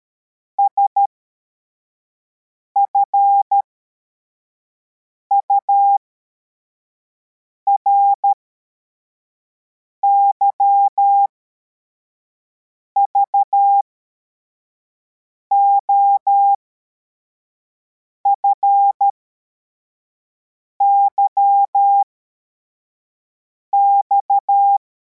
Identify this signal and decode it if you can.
Morse code, heard as a keyed tone, about 13 wpm, spaced as for 4 wpm, 800 Hz. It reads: SFURY VOFYX
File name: formation / cw / Text61